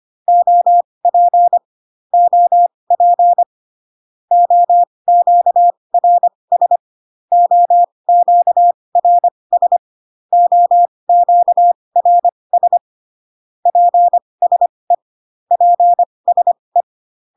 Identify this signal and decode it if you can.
Morse code, heard as a keyed tone, about 25 wpm, spaced as for 20 wpm, 700 Hz. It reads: OP OP OQRS OQRS OQRS PSE PSE